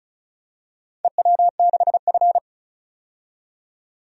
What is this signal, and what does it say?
Morse code, heard as a keyed tone, about 36 wpm, 700 Hz. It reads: EW6F
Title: DXCC-100 Call Signs - 4 Characters - 35wpm